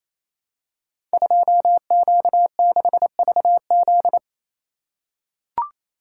Morse code, heard as a keyed tone, about 28 wpm, 690 Hz. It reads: 2Q6VZ
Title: Code Groups - 5 Letter:Numbers - 28wpm